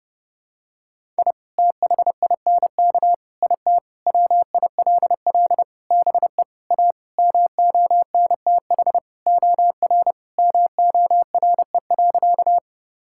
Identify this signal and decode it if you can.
Morse code, heard as a keyed tone, about 32 wpm, 700 Hz. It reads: I THINK IT WILL BE A MONTH OR MORE.
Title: Sentences from Top 1000 Words - Review Part 5 - 30wpm